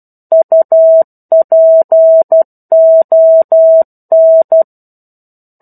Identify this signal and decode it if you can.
Morse code, heard as a keyed tone, about 12 wpm, 650 Hz. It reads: UPON
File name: 01-Chapter-01 12 WPM